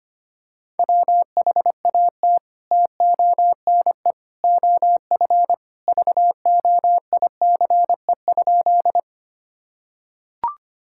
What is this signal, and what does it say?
Morse code, heard as a keyed tone, about 25 wpm, 700 Hz. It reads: WHAT TONE OF VOICE?
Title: Sentences from Top 1600 Words 25wpm